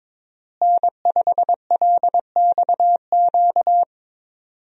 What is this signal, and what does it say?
Morse code, heard as a keyed tone, about 22 wpm, 700 Hz. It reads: N5LXQ
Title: USA Call Signs - 5 Characters - Repeated 2x - 22wpm